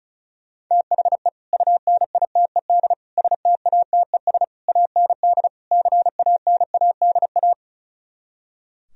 Morse code, heard as a keyed tone, about 43 wpm, 700 Hz. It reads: THE UNITED STATES AND CANADA
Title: Speed-Racing - Sets of 5 Words 35wpm